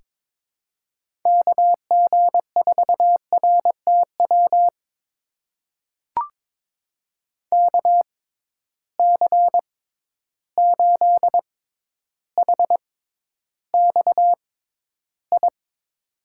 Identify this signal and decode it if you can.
Morse code, heard as a keyed tone, about 22 wpm, spaced as for 11 wpm, 700 Hz. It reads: KG4RTW KC8HXI
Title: USA Call Signs - 6 Characters - 6x Character Spacing - 22wpm